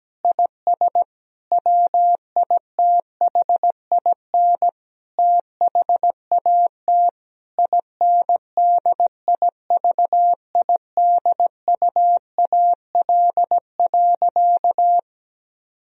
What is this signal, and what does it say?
Morse code, heard as a keyed tone, about 17 wpm, 700 Hz. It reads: IS WITHIN THAT INDIVIDUAL.
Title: Sentences from Top 1000 Words - Review Part 4 - 17wpm